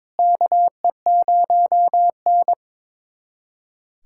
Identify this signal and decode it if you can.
Morse code, heard as a keyed tone, about 22 wpm, 700 Hz. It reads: KE0N
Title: USA Callsign DE Callsign - 4 Characters - 22wpm